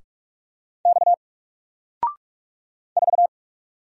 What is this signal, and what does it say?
Morse code, heard as a keyed tone, about 44 wpm, 700 Hz. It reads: X 4
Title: Single Character (Letter:Number:Punctuation:Prosign) 45wpm